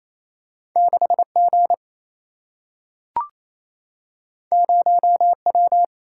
Morse code, heard as a keyed tone, about 28 wpm, 700 Hz. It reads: 6 G 0 W
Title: Single-Letter Single-Number 28wpm